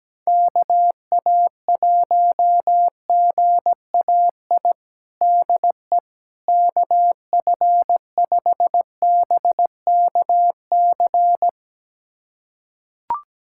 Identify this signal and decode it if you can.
Morse code, heard as a keyed tone, about 17 wpm, 700 Hz. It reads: KA1GAI DE KF5BKC
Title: USA Callsign DE Callsign - 6 Characters - 17wpm